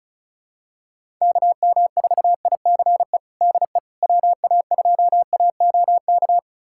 Morse code, heard as a keyed tone, about 34 wpm, 700 Hz. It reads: KM4ICE DE WA2AOK
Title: USA Callsign DE Callsign - 6 Characters - 35wpm